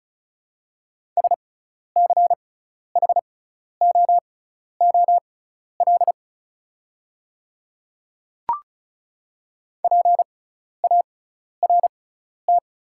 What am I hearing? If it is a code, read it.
Morse code, heard as a keyed tone, about 36 wpm, spaced as for 12 wpm, 690 Hz. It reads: SCHOOL PART
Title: 6x Character Spacing - Top 300 Words - 35wpm